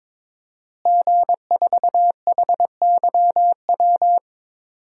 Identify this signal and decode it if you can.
Morse code, heard as a keyed tone, about 23 wpm, 700 Hz. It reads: G4HYW
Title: DXCC-100 Call Signs - 5 Characters - Repeated 3x - 22wpm